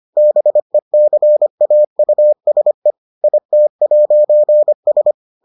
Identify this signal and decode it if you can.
Morse code, heard as a keyed tone, about 25 wpm, 600 Hz. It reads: BECAUSE IT'S